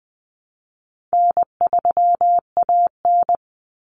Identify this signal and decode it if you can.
Morse code, heard as a keyed tone, about 20 wpm, 700 Hz. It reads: N3AN
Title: Common Cut Numbers - 4-Digit 20wpm